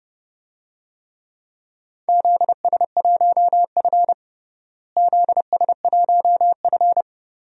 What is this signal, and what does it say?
Morse code, heard as a keyed tone, about 32 wpm, 700 Hz. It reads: ZS1F ZS1F
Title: DXCC-100 Call Signs - 4 Characters - Repeated 3x - 30wpm